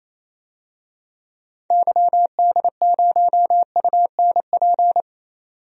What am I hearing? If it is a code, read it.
Morse code, heard as a keyed tone, about 28 wpm, 700 Hz. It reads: YD0UNP
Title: DXCC-100 Call Signs - 6 Characters - 28wpm